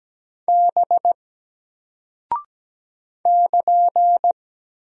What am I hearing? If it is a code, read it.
Morse code, heard as a keyed tone, about 17 wpm, 700 Hz. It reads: B <KN>
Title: Single Character (Letter:Number:Punctuation:Prosign) Alt 17wpm